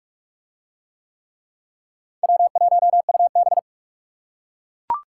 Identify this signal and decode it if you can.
Morse code, heard as a keyed tone, about 42 wpm, 700 Hz. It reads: W1UB